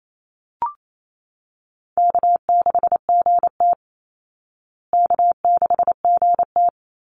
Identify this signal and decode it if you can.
Morse code, heard as a keyed tone, about 28 wpm, 700 Hz. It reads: K6GT K6GT
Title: Easy Call Signs Repeated 3x - 28wpm